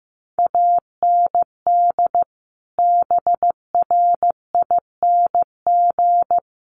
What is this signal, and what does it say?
Morse code, heard as a keyed tone, about 15 wpm, 700 Hz. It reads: AND BRING
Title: Sets of 5 Words Encore 15wpm